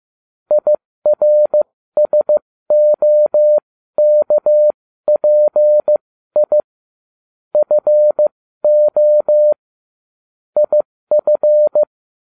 Morse code, heard as a keyed tone, about 15 wpm, spaced as for 13 wpm, 600 Hz. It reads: IRSOKPI FO IF